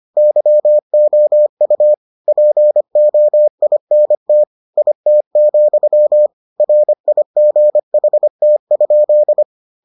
Morse code, heard as a keyed tone, about 25 wpm, 600 Hz. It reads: YOU POINT IT, RIGHT?